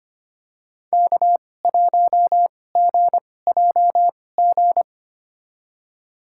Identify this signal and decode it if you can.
Morse code, heard as a keyed tone, about 25 wpm, spaced as for 18 wpm, 700 Hz. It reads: K1GJG
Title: USA Call Signs - 5 Characters - 2x Character Spacing - 25wpm